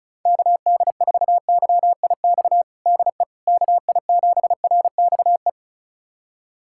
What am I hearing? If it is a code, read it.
Morse code, heard as a keyed tone, about 34 wpm, 700 Hz. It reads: KD4YIX DE KI7RXE
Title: USA Callsign DE Callsign - 6 Characters - 35wpm